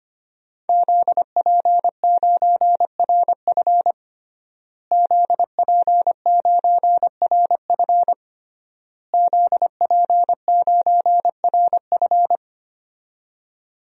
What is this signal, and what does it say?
Morse code, heard as a keyed tone, about 25 wpm, 700 Hz. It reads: ZP9RF ZP9RF ZP9RF
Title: DXCC-100 Call Signs - 5 Characters - Repeated 3x - 25wpm